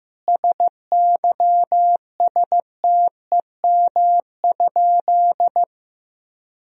Morse code, heard as a keyed tone, about 15 wpm, 700 Hz. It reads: SYSTEM?
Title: Sentences from Top 1000 Words - Review Part 3 - 15wpm